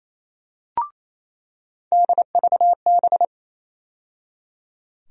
Morse code, heard as a keyed tone, about 28 wpm, 700 Hz. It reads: DVB
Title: USA Call Sign Postfixes - 3 Letters 28wpm